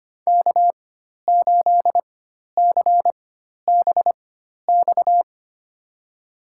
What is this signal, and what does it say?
Morse code, heard as a keyed tone, about 25 wpm, spaced as for 12 wpm, 700 Hz. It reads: K8CBX
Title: USA Call Signs - 4x Character Spacing - 25wpm